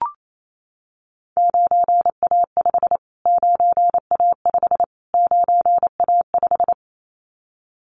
Morse code, heard as a keyed tone, about 28 wpm, 700 Hz. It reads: 9A5 9A5 9A5
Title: Callsign Prefixes for Top 25 DXCC - Repeated 3x - 28wpm